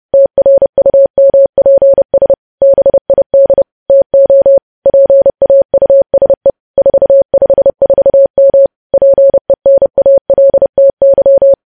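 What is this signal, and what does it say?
Morse code, heard as a keyed tone, about 30 wpm, 550 Hz. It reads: TRUMPS BID TO PAUSE 454M PENALTY
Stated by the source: News headlines sent in Morse at 30 wpm, preceded by VVV.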